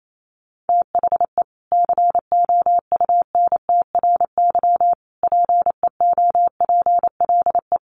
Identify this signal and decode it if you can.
Morse code, heard as a keyed tone, about 28 wpm, 700 Hz. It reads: THE COUNTRY PEOPLE
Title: Sentences from Top 1100 Words 28wpm